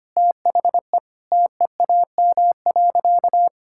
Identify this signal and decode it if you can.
Morse code, heard as a keyed tone, about 25 wpm, 700 Hz. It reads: THE TEAM.
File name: practice / 25wpm / Sentences with 4-Word Phrases - 25wpm